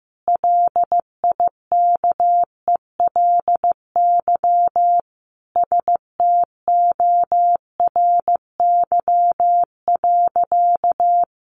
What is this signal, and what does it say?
Morse code, heard as a keyed tone, about 15 wpm, 700 Hz. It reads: LIKELY STORY.
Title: Sentences from Top 600 Words 15wpm